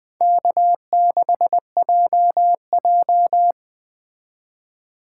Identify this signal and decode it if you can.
Morse code, heard as a keyed tone, about 20 wpm, 700 Hz. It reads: K6JJ
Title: Speed Racing - USA Call Signs 17wpm